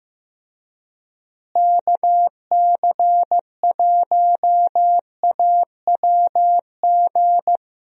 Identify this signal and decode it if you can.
Morse code, heard as a keyed tone, about 16 wpm, 700 Hz. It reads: KC1AWG
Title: USA Call Signs - 6 Characters - Repeated 3x - 15wpm